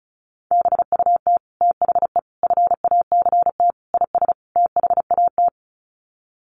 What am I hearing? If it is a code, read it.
Morse code, heard as a keyed tone, about 34 wpm, 700 Hz. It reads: BUT THE FACT IS THAT
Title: Sets of 5 Words Encore2 35wpm